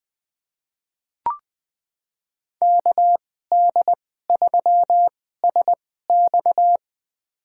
Morse code, heard as a keyed tone, about 20 wpm, spaced as for 15 wpm, 700 Hz. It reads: KD3SX
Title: USA Call Signs - 5 Characters - 2x Character Spacing - 20wpm